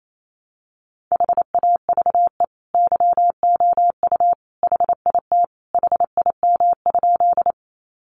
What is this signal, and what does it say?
Morse code, heard as a keyed tone, about 28 wpm, 700 Hz. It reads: HAVE YOU HIT HIM?
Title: Sentences from Top 1200 Words 28wpm